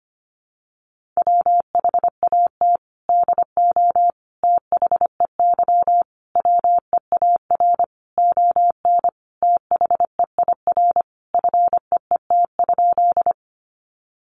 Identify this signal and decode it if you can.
Morse code, heard as a keyed tone, about 25 wpm, 700 Hz. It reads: WHAT DO THEY WEAR ON THEIR FEET?
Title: Sentences from Top 1500 Words 25wpm